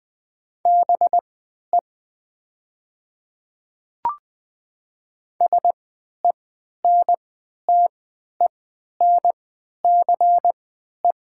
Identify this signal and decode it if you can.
Morse code, heard as a keyed tone, about 20 wpm, spaced as for 11 wpm, 700 Hz. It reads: BE SENTENCE
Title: 3x Character Spacing - Top 200 Words - 20wpm